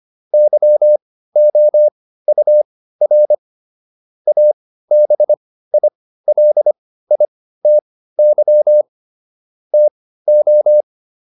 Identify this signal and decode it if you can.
Morse code, heard as a keyed tone, about 25 wpm, spaced as for 15 wpm, 600 Hz. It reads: YOUR ABILITY TO